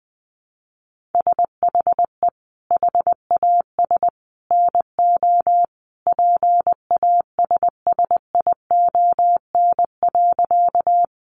Morse code, heard as a keyed tone, about 20 wpm, 700 Hz. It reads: SHE HAS NO PASSION.
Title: Sentences from Top 1300 Words 20wpm